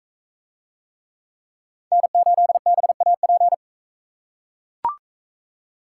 Morse code, heard as a keyed tone, about 44 wpm, 700 Hz. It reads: N8BAP